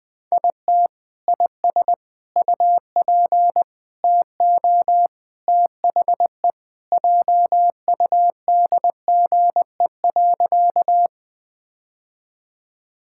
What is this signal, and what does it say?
Morse code, heard as a keyed tone, about 20 wpm, 700 Hz. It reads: IT IS UP TO THE JUDGE.
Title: Sentences with 3-Word Phrases - Spoken Beginning - 20wpm